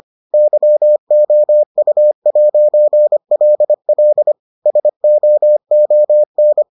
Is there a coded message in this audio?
Yes: Morse code from 0.3 to 6.6 s, heard as a keyed tone, about 25 wpm, 600 Hz. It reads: YOU'LL SOON